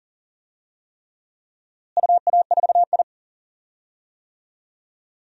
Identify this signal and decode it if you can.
Morse code, heard as a keyed tone, about 40 wpm, 700 Hz. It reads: UA4I
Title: DXCC-100 Call Signs - 4 Characters - Repeated 3x - 40wpm